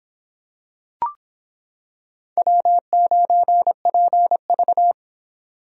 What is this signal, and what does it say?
Morse code, heard as a keyed tone, about 26 wpm, 700 Hz. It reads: W9PV